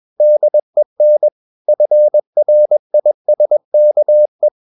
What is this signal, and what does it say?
Morse code, heard as a keyed tone, about 21 wpm, 600 Hz. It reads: DEN FRISKE
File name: Ferie QRS 21wpm